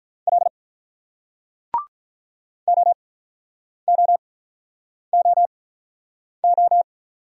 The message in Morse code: L OOOO

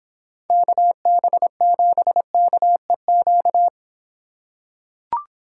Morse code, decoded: KB7KEQ

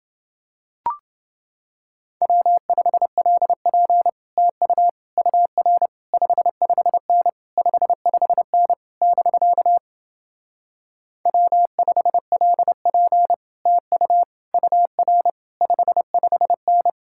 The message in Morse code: W5LP TU UR 55N 55N <BK> W5LP TU UR 55N

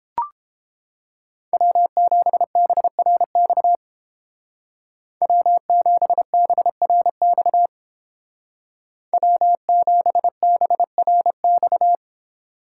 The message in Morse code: W7BRX W7BRX W7BRX

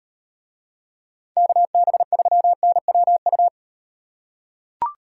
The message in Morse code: KB3NWU

